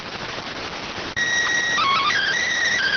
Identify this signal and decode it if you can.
SSTV (slow-scan television), PD 290.